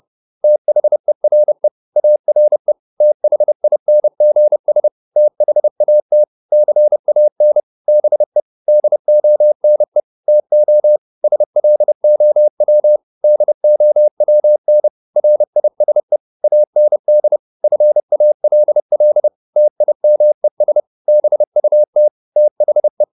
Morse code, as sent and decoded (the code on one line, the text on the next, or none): - .... .-. / .- .-. / - .... .. -. --. ... / - .... .- - / -.-. .- -. / -... / -.. --- -. / - --- / ... .-.. --- .-- / -.. --- .-- -. / .-. .. ... / .- -. -.. / ..-. .- .-.. .-.. / - .. -- . ... / -... ..- - / - .... .
THERE ARE THINGS THAT CAN BE DONE TO SLOW DOWN RISE AND FALL TIMES BUT THE